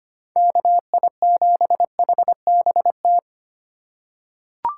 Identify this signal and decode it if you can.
Morse code, heard as a keyed tone, about 25 wpm, 700 Hz. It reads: KI7HBT
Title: USA Call Signs - 6 Characters - 2x Character Spacing - 25wpm